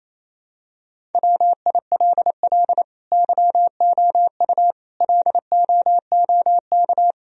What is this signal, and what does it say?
Morse code, heard as a keyed tone, about 28 wpm, 700 Hz. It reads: WILL YOU LOOK
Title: Sentences from Top 1000 Words - Review Part 1 - 28wpm